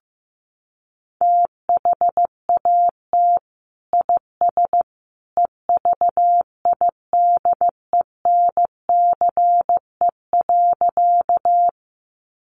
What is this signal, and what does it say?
Morse code, heard as a keyed tone, about 15 wpm, 700 Hz. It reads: THAT IS EVIDENCE.
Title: Sentences from Top 1300 Words 15wpm